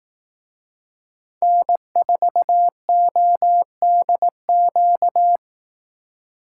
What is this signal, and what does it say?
Morse code, heard as a keyed tone, about 18 wpm, 700 Hz. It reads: N4ODQ
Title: Speed Racing - USA 1x3 Call Signs 15wpm